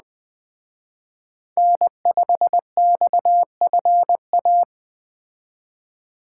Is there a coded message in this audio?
Yes: Morse code from 1.6 to 4.6 s, heard as a keyed tone, about 21 wpm, 700 Hz. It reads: N5XFA